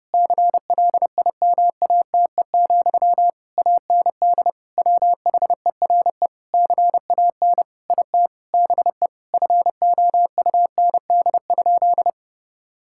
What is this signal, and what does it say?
Morse code, heard as a keyed tone, about 32 wpm, 700 Hz. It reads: CLIMATE, AND WHERE CAN IT BE FOUND?
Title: Sentences from Top 1600 Words 30wpm